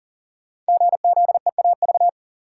Morse code, heard as a keyed tone, about 41 wpm, 700 Hz. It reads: GZEAV
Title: Code Groups - 5 Letter:Numbers - 40wpm